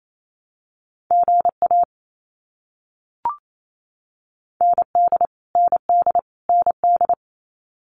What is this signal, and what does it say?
Morse code, heard as a keyed tone, about 28 wpm, 700 Hz. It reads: GA ND ND ND
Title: US States Repeated 3x 28wpm